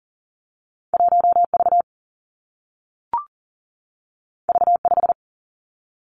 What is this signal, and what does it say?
Morse code, heard as a keyed tone, about 42 wpm, 700 Hz. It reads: 1 V V 5
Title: Two Challenging Letter-Number Combinations 40wpm